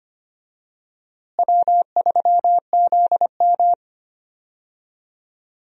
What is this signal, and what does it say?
Morse code, heard as a keyed tone, about 25 wpm, 700 Hz. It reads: W3ZM
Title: USA Call Signs - 4 Characters - Repeated 3x - 25wpm